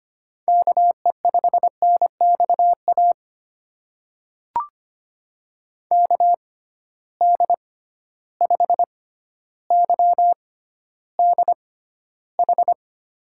Morse code, heard as a keyed tone, about 25 wpm, spaced as for 13 wpm, 700 Hz. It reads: KE5NXA KD5YDH
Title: USA Call Signs - 6 Characters - 6x Character Spacing - 25wpm